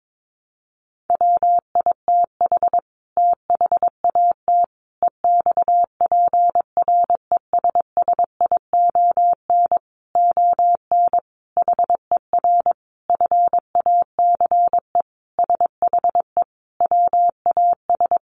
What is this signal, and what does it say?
Morse code, heard as a keyed tone, about 22 wpm, 700 Hz. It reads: WITH THAT EXPRESSION ON HER FACE SHE WAS
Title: Sentences from Top 1200 Words 22wpm